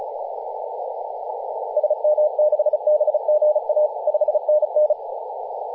Morse code, heard as a keyed tone, about 36 wpm, 600 Hz, with radio noise. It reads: SM6BGA 5NN
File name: posts / 2016-nov-14-0725z-tl8ao-14-023mhz-cw